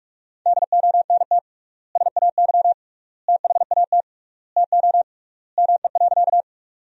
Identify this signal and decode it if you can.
Morse code, heard as a keyed tone, about 43 wpm, 700 Hz. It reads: DONT SAY THAT TO ME.
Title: 3x Word-Spacing - Sentences from Top 300 Words - 45wpm